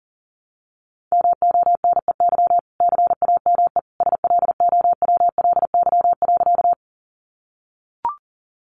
Morse code, heard as a keyed tone, about 40 wpm, 700 Hz. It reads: MONEY CAME SLOWLY.
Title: Sentences from Top 1100 Words 40wpm